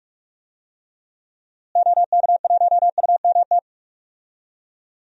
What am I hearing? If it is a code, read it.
Morse code, heard as a keyed tone, about 45 wpm, 700 Hz. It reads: OK1UMT
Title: DXCC-100 Call Signs - 6 Characters - Repeated 3x - 45wpm